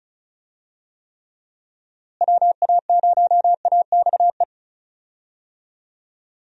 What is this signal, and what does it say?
Morse code, heard as a keyed tone, about 36 wpm, 700 Hz. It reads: WA0AXE
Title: USA Call Signs - 6 Characters - Repeated 2x - 35wpm